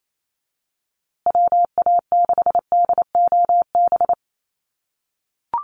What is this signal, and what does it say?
Morse code, heard as a keyed tone, about 28 wpm, 700 Hz. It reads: WA6DOB
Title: USA Call Signs Repeated 2x 28wpm